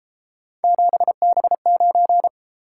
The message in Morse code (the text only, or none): 7B9